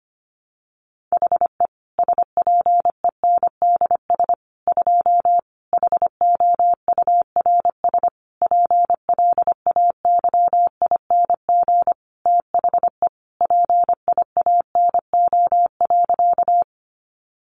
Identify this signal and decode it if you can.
Morse code, heard as a keyed tone, about 25 wpm, 700 Hz. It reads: HE SPENDS 2 HOURS PLAYING THE PIANO.
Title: Additional Easy Phrases 25wpm